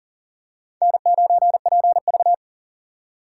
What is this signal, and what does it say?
Morse code, heard as a keyed tone, about 40 wpm, 700 Hz. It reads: N9PV
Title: USA Call Signs - 4 Characters - 2x Character Spacing - 40wpm